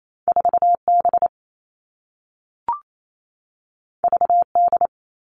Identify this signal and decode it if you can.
Morse code, heard as a keyed tone, about 28 wpm, 700 Hz. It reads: 4B VD
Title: All Cut Numbers - 2-Digit 28wpm